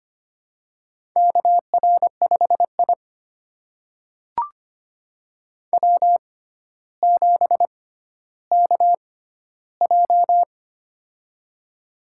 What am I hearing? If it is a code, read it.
Morse code, heard as a keyed tone, about 25 wpm, spaced as for 13 wpm, 700 Hz. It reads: KR5I W7KJ
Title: USA Call Signs - 4 Characters - 6x Character Spacing - 25wpm